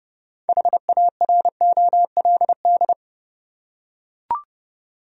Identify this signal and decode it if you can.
Morse code, heard as a keyed tone, about 31 wpm, 700 Hz. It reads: HAROLD
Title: USA Common Names - 30wpm